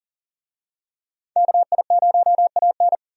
Morse code, heard as a keyed tone, about 40 wpm, 700 Hz. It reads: KI0AN
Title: USA Call Signs - 5 Characters - 40wpm